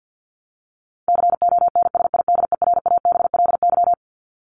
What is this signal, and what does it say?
Morse code, heard as a keyed tone, about 49 wpm, 700 Hz. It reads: CONSIDERABLY
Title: LY Suffix - Top 100 Words with Base Words 50wpm